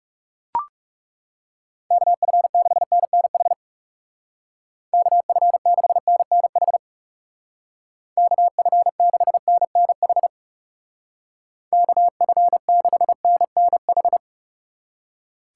Morse code, heard as a keyed tone, about 40 wpm, 700 Hz. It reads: KF6NNH KF6NNH KF6NNH KF6NNH